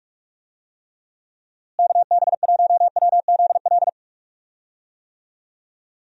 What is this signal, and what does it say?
Morse code, heard as a keyed tone, about 46 wpm, 700 Hz. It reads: KB1WZL
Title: USA Call Signs - 6 Characters - Repeated 2x - 45wpm